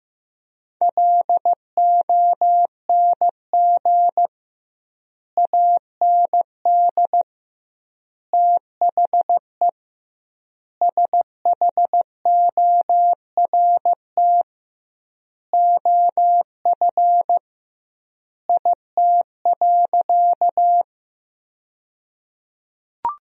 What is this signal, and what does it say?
Morse code, heard as a keyed tone, about 15 wpm, 700 Hz. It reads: LONG AND THE SHORT OF IT.
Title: 2x Word-Spacing - Sentences from Top 400 Words - 15wpm